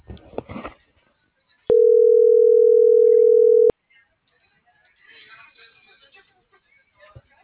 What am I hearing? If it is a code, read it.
Ringback tone (North America), 440 + 480 Hz.